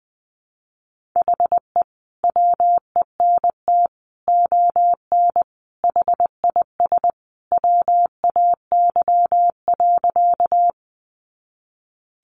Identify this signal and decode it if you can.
Morse code, heard as a keyed tone, about 20 wpm, 700 Hz. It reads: HE WENT ON HIS WAY.
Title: Sentences from Top 200 Words 20wpm